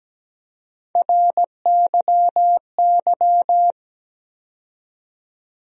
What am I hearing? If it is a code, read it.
Morse code, heard as a keyed tone, about 17 wpm, 700 Hz. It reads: RYY
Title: USA Call Sign Postfixes - 3 Letters 17wpm